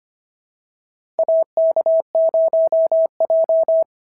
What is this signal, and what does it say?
Morse code, heard as a keyed tone, about 25 wpm, 650 Hz. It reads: AK0J
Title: SOTA QSO Practice 25wpm